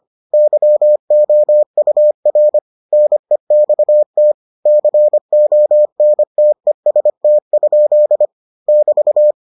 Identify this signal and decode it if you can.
Morse code, heard as a keyed tone, about 25 wpm, 600 Hz. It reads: YOUR NEXT CONTEST? <BT>